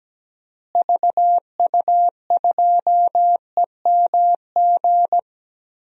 Morse code, heard as a keyed tone, about 17 wpm, 700 Hz. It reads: VU2EMG
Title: DXCC-100 Call Signs - 6 Characters - Repeated 2x - 17wpm